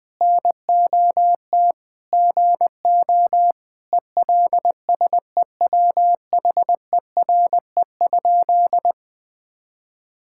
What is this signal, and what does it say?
Morse code, heard as a keyed tone, about 20 wpm, 700 Hz. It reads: NOT GO ELSEWHERE?
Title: Sentences from Top 1600 Words 20wpm